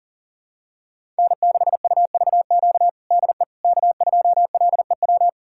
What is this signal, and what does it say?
Morse code, heard as a keyed tone, about 40 wpm, 700 Hz. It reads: N6UVQ DE K2LEW
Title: USA Callsign DE Callsign - 5 Characters - 40wpm